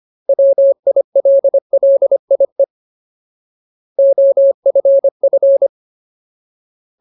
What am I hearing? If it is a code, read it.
Morse code, heard as a keyed tone, about 25 wpm, 550 Hz. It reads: WILLIE OFF
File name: Ultimate Word List / 25WPM Ultimate